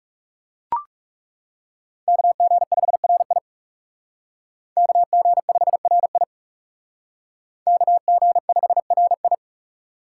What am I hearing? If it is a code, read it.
Morse code, heard as a keyed tone, about 38 wpm, 700 Hz. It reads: KG5RI KG5RI KG5RI